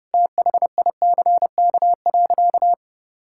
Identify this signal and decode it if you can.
Morse code, heard as a keyed tone, about 30 wpm, 700 Hz. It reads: THICK.
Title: Sentences from Top 1600 Words 30wpm